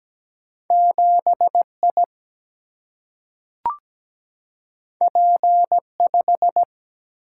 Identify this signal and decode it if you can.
Morse code, heard as a keyed tone, about 17 wpm, 700 Hz. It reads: 7 I P 5
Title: Single-Letter Single-Number 17wpm